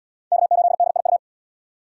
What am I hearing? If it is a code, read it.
Morse code, heard as a keyed tone, about 71 wpm, 700 Hz. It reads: K8DIU